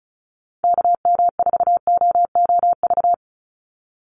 Morse code, heard as a keyed tone, about 34 wpm, 700 Hz. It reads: KM4OOV
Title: USA Call Signs 35wpm